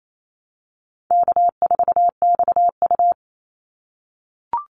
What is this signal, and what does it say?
Morse code, heard as a keyed tone, about 28 wpm, 700 Hz. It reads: K4XU
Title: Easy Call Signs - 28wpm